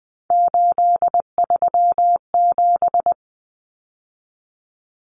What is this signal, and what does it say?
Morse code, heard as a keyed tone, about 20 wpm, 700 Hz. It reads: 837
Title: Numbers - 3-Digit 20wpm